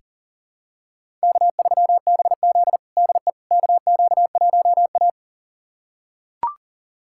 Morse code, heard as a keyed tone, about 37 wpm, 700 Hz. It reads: K3BZ DE KQ1A
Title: USA Callsign DE Callsign - 4 Characters - 40wpm